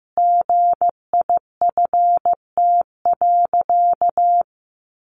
Morse code, heard as a keyed tone, about 15 wpm, 700 Hz. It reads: GIFT.
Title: Sentences from Top 1400 Words 15wpm